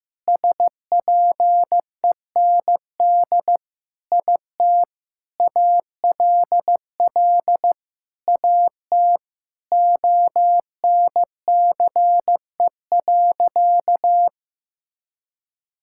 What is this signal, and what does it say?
Morse code, heard as a keyed tone, about 15 wpm, 700 Hz. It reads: SPEND IT ALL AT ONCE.
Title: Sentences from Top 1000 Words - Review Part 5 - 15wpm